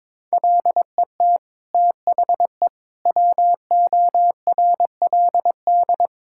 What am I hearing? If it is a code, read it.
Morse code, heard as a keyed tone, about 22 wpm, 700 Hz. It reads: LET THE WORLD
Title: Sentences from Top 1600 Words 22wpm